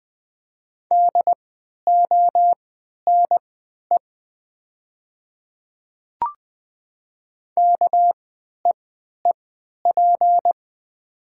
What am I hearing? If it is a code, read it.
Morse code, heard as a keyed tone, about 20 wpm, spaced as for 11 wpm, 700 Hz. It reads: DONE KEEP
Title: 3x Character Spacing - Top 500 Words - 20wpm